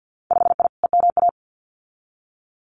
Morse code, heard as a keyed tone, about 50 wpm, 700 Hz. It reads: 5I EMA
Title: Winter Field Day Exchanges Encore 50wpm